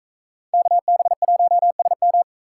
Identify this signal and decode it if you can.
Morse code, heard as a keyed tone, about 44 wpm, 700 Hz. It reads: KB1SM